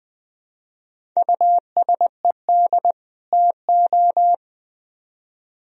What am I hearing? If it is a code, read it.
Morse code, heard as a keyed tone, about 23 wpm, 700 Hz. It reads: USED TO
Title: Speed-Racing - Sets of 2 Words 20wpm